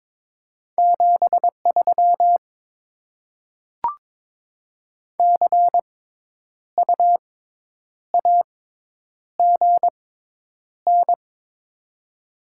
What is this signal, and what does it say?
Morse code, heard as a keyed tone, about 22 wpm, spaced as for 9 wpm, 700 Hz. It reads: 73 CU AGN
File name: Top 20 QSO Elements - 6x Character Spacing - 22wpm